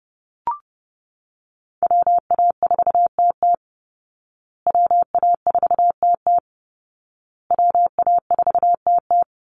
Morse code, heard as a keyed tone, about 30 wpm, 700 Hz. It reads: WA4TT WA4TT WA4TT
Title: Call Signs Repeated 2x - 30wpm